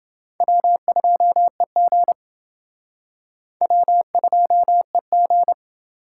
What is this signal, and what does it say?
Morse code, heard as a keyed tone, about 29 wpm, 700 Hz. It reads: W2EG W2EG